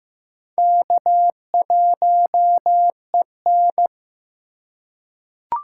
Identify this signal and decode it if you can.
Morse code, heard as a keyed tone, about 15 wpm, 700 Hz. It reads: K1EN
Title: USA Call Signs - 4 Characters - 3x Character Spacing - 15wpm